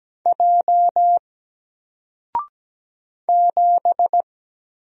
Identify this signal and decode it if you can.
Morse code, heard as a keyed tone, about 17 wpm, 700 Hz. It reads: J 7
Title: Single Character (Letter:Number:Punctuation:Prosign) Alt 17wpm